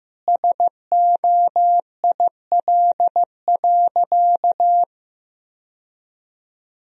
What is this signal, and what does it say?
Morse code, heard as a keyed tone, about 15 wpm, 700 Hz. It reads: SOIL.
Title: Sentences from Top 1000 Words - Review Part 4 - 15wpm